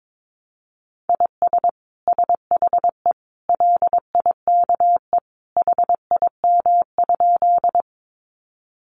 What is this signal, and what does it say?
Morse code, heard as a keyed tone, about 22 wpm, 700 Hz. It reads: IS SHE LIKE HIM?
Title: Sentences from Top 100 Words 22wpm